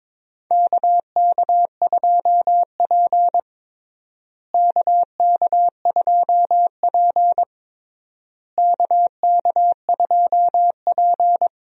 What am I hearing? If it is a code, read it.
Morse code, heard as a keyed tone, about 22 wpm, 700 Hz. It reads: KK2P KK2P KK2P
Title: USA Call Signs - 4 Characters - Repeated 3x - 22wpm